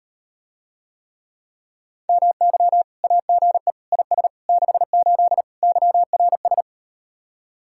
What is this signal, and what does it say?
Morse code, heard as a keyed tone, about 38 wpm, 700 Hz. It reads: MY AGE IS 68 YRS